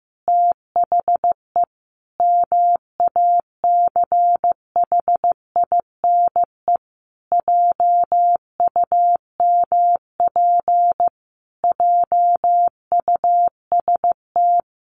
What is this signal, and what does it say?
Morse code, heard as a keyed tone, about 15 wpm, 700 Hz. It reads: THE MACHINE JUMP JUST
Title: A Sound Of Thunder 15wpm